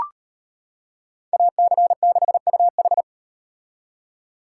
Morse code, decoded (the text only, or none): AC6UH